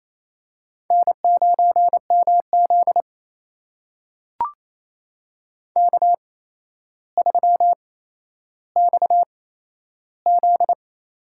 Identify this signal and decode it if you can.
Morse code, heard as a keyed tone, about 28 wpm, spaced as for 12 wpm, 700 Hz. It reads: N9MZ K3XZ
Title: USA Call Signs - 4 Characters - 8x Character Spacing - 28wpm